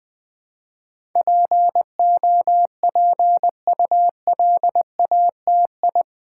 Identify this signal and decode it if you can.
Morse code, heard as a keyed tone, about 20 wpm, 700 Hz. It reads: POPULATI
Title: Build A Word - 4 Letters At A Time - 10-Letter Words - 20wpm